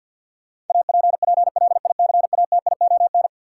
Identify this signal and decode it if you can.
Morse code, heard as a keyed tone, about 53 wpm, 700 Hz. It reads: APPLICATION
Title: Sentences from Top 1200 Words - New Word Drill Alt - 50wpm